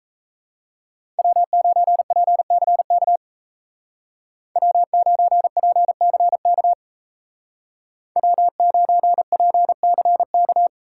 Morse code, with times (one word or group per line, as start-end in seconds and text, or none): 1.2-3.2 W9PCK
4.6-6.7 W9PCK
8.2-10.7 W9PCK